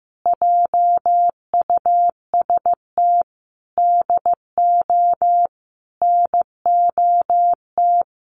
Morse code, heard as a keyed tone, about 15 wpm, 700 Hz. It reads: JUST DO NOT
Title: Sets of 5 Words Encore 15wpm